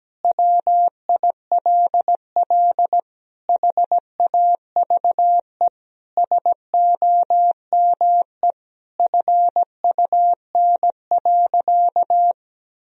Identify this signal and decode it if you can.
Morse code, heard as a keyed tone, about 17 wpm, 700 Hz. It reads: WILL HAVE SOME FUN.
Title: Sentences from Top 1000 Words - Review Part 4 - 17wpm